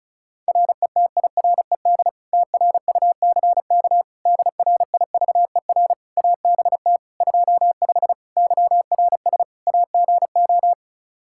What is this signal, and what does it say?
Morse code, heard as a keyed tone, about 36 wpm, 700 Hz. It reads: RETIRED TRUCK DRIVER ABT 25 YRS AGO
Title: Practice QSO - Exchange 3 - More Info - 35wpm